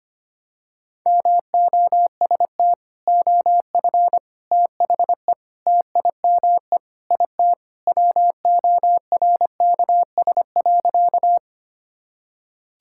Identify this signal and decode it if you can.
Morse code, heard as a keyed tone, about 25 wpm, 700 Hz. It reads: MOST OF THE TIME IT WORKS.
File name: practice / 25wpm / Sentences with 4-Word Phrases - 25wpm